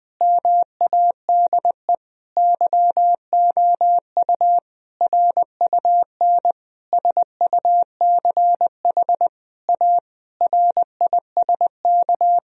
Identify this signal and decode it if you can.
Morse code, heard as a keyed tone, about 21 wpm, 700 Hz. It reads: MADE YOU RUN SUCH A RISK
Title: Sentences from Top 1000 Words - Review Part 4 - 20wpm